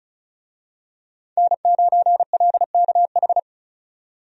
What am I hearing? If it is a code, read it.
Morse code, heard as a keyed tone, about 36 wpm, 700 Hz. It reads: N9LKH
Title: USA Call Signs - 5 Characters - 35wpm